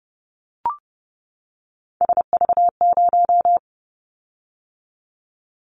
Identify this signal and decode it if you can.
Morse code, heard as a keyed tone, about 30 wpm, 700 Hz. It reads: SV0
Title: Callsign Prefixes for Top 50 DXCC - 30wpm